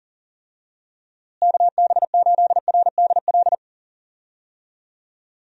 Morse code, heard as a keyed tone, about 39 wpm, 700 Hz. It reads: KB8RDL
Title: USA Call Signs - 6 Characters - Repeated 2x - 40wpm